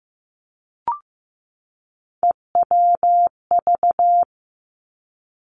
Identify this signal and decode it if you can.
Morse code, heard as a keyed tone, about 15 wpm, 700 Hz. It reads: EWV
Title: Three Challenging Letter-Number Combinations 15wpm